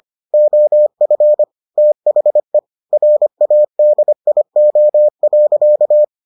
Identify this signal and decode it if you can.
Morse code, heard as a keyed tone, about 25 wpm, 600 Hz. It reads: OF THE RADIO.